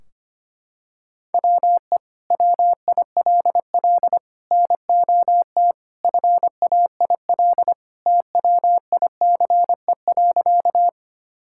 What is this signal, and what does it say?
Morse code, heard as a keyed tone, about 25 wpm, 700 Hz. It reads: WE WILL NOT FAIL TWICE.
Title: Sentences from Top 1600 Words 25wpm